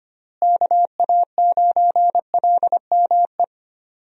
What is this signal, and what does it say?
Morse code, heard as a keyed tone, about 25 wpm, 700 Hz. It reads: KA9LME
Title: USA Call Signs - 6 Characters - Repeated 3x - 25wpm